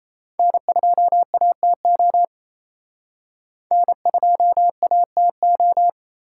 Morse code, heard as a keyed tone, about 38 wpm, 700 Hz. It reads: N2ATO N2ATO